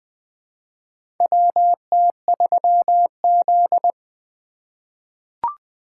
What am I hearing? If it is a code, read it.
Morse code, heard as a keyed tone, about 20 wpm, 700 Hz. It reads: WT3Z
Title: USA Call Signs - 2x Character Spacing - 20wpm